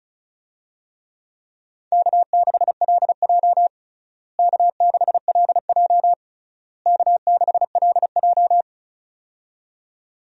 Morse code, heard as a keyed tone, about 36 wpm, 700 Hz. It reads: K6LJ K6LJ K6LJ
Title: USA Call Signs - 4 Characters - Repeated 3x - 35wpm